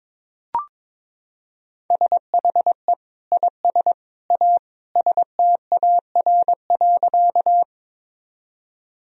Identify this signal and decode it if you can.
Morse code, heard as a keyed tone, about 21 wpm, 700 Hz. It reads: SHE IS A STAR.
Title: Sentences from Top 1000 Words - Review Part 2 - 22wpm